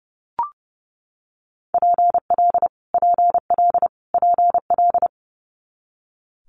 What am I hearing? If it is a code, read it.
Morse code, heard as a keyed tone, about 30 wpm, 700 Hz. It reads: PL PL PL
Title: Multi-letter Phonemes Repeated 3x 30wpm